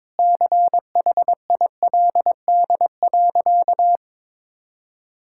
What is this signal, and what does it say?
Morse code, heard as a keyed tone, about 22 wpm, 700 Hz. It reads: CHILD.
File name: practice / 22wpm / Sentences with 4-Word Phrases - 22wpm